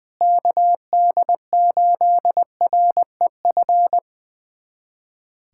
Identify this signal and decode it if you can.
Morse code, heard as a keyed tone, about 20 wpm, 700 Hz. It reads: KD8REF
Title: USA Callsign DE Callsign 20wpm